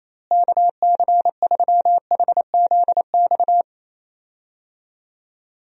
Morse code, heard as a keyed tone, about 28 wpm, 700 Hz. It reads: KC3HZX
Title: USA Call Signs - 6 Characters - Repeated 3x - 28wpm